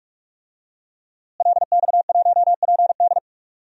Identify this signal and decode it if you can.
Morse code, heard as a keyed tone, about 46 wpm, 700 Hz. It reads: LX1PD
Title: DXCC-100 Call Signs - 5 Characters - Repeated 3x - 45wpm